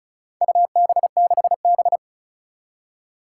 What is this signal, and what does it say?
Morse code, heard as a keyed tone, about 36 wpm, 700 Hz. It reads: UB6B
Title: Four Challenging Letter-Number Combinations 35wpm